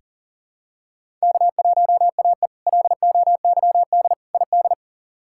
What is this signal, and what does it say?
Morse code, heard as a keyed tone, about 40 wpm, 700 Hz. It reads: K1AE LOYD ID
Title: K1USN SST QSO Practice - 40wpm